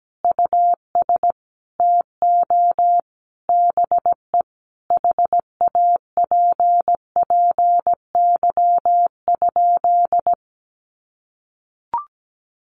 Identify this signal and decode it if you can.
Morse code, heard as a keyed tone, about 17 wpm, 700 Hz. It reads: US TO BE HAPPY?
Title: Sentences from Top 1100 Words 17wpm